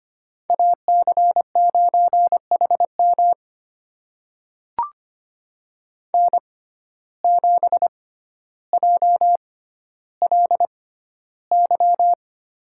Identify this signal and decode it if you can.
Morse code, heard as a keyed tone, about 25 wpm, spaced as for 13 wpm, 700 Hz. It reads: AC9HM N7JLY
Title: USA Call Signs - 5 Characters - 6x Character Spacing - 25wpm